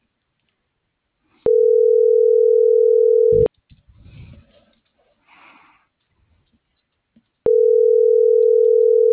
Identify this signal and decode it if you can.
Ringback tone (North America), 440 + 480 Hz.